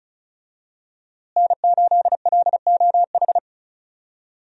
Morse code, heard as a keyed tone, about 34 wpm, 700 Hz. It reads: N8LOH
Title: USA Call Signs - 5 Characters - Repeated 2x - 35wpm